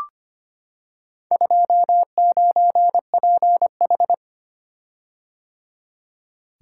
Morse code, heard as a keyed tone, about 25 wpm, 700 Hz. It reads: 29PH
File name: practice / 25wpm / Four Letter-Number Combinations - 25wpm